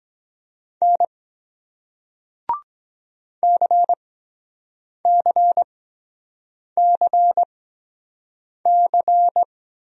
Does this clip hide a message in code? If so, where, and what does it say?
Yes: Morse code from 0.8 to 9.4 s, heard as a keyed tone, about 23 wpm, spaced as for 7 wpm, 700 Hz. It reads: N CCCC